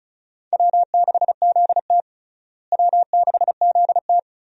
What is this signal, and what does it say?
Morse code, heard as a keyed tone, about 36 wpm, 700 Hz. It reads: W6ZT W6ZT
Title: USA Call Signs - 4 Characters - Repeated 2x - 35wpm